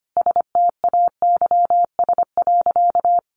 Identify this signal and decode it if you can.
Morse code, heard as a keyed tone, about 25 wpm, 700 Hz. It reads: STAYS.
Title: A Sound Of Thunder 25wpm